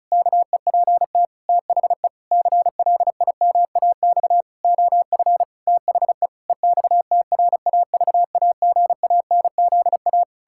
Morse code, heard as a keyed tone, about 36 wpm, 700 Hz. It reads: KEPT THE CLIMAX OF THE EXTRAVAGANZA
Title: Pangram Challenge v2 - 35wpm